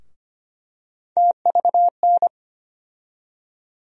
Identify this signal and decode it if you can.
Morse code, heard as a keyed tone, about 25 wpm, 700 Hz. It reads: TVN
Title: USA Call Sign Postfixes - 3 Letters 25wpm